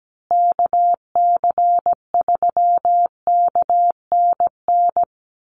Morse code, heard as a keyed tone, about 17 wpm, 700 Hz. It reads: KC3KNN
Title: USA Call Signs Repeated 3x 17wpm